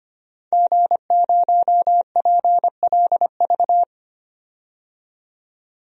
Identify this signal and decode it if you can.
Morse code, heard as a keyed tone, about 25 wpm, 700 Hz. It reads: G0PLV
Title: DXCC-100 Call Signs - 5 Characters - 25wpm